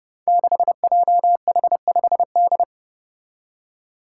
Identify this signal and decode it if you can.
Morse code, heard as a keyed tone, about 30 wpm, 700 Hz. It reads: 6JH5D
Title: Five Challenging Letter-Number Combinations 30wpm